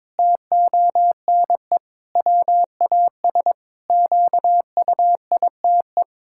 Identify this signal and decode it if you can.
Morse code, heard as a keyed tone, about 22 wpm, 700 Hz. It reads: TONE WAS QUITE
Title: Sentences from Top 1600 Words 22wpm